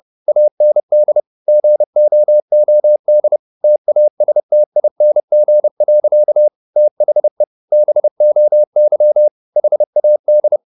Morse code, heard as a keyed tone, about 30 wpm, 600 Hz. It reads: AND GOOD TASTING. THE BOY HAD
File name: the old man and the sea / the old man and the sea 1 30wpm